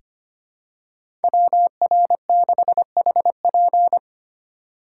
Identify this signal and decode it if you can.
Morse code, heard as a keyed tone, about 25 wpm, 700 Hz. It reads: WR6HP
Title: USA Call Signs - 5 Characters - 4x Character Spacing - 25wpm